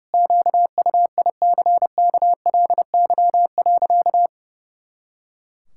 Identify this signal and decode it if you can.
Morse code, heard as a keyed tone, about 32 wpm, 700 Hz. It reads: QUICKLY.
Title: Pangram Challenge v2 - 30wpm